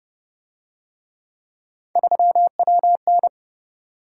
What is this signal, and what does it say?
Morse code, heard as a keyed tone, about 28 wpm, 700 Hz. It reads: 3WN